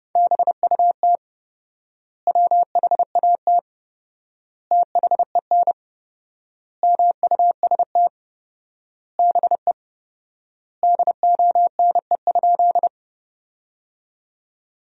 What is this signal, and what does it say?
Morse code, heard as a keyed tone, about 30 wpm, 700 Hz. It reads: BUT WHAT THEN MUST BE DONE?
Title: 4x Word-Spacing - Sentences from Top 500 Words - 30wpm